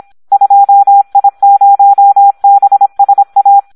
Morse code, heard as a keyed tone, about 26 wpm, 800 Hz. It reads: 2I0BSA
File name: Audio practica CW / SD 1 26